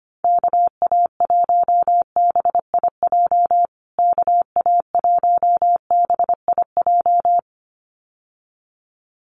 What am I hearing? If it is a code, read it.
Morse code, heard as a keyed tone, about 25 wpm, 700 Hz. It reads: KA1BIJ KA1BIJ
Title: USA Call Signs Repeated 2x 25wpm